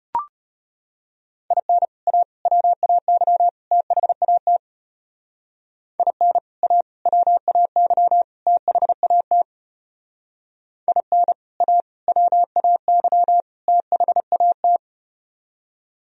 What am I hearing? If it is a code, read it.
Morse code, heard as a keyed tone, about 33 wpm, 700 Hz. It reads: IN A WAY THAT IN A WAY THAT IN A WAY THAT